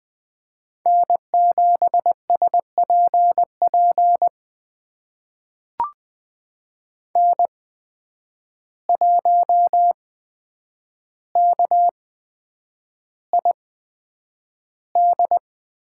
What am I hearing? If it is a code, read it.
Morse code, heard as a keyed tone, about 20 wpm, spaced as for 9 wpm, 700 Hz. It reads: N7SPP N1KID
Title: USA Call Signs - 5 Characters - 8x Character Spacing - 20wpm